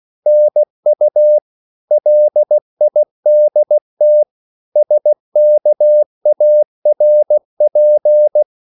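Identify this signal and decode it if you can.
Morse code, heard as a keyed tone, about 16 wpm, 600 Hz. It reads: NU LIDT SKARP
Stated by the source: Slow (QRS) Morse code at 16 wpm, from CW med Gnister.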